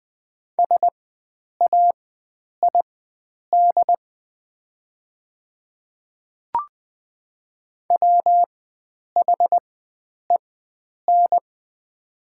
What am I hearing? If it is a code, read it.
Morse code, heard as a keyed tone, about 20 wpm, spaced as for 9 wpm, 700 Hz. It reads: SAID WHEN
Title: 4x Character Spacing - Top 300 Words - 20wpm